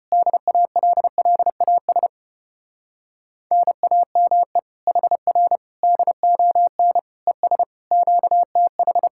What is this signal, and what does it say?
Morse code, heard as a keyed tone, about 33 wpm, 700 Hz. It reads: DALLAS NAME HR DON ES QTH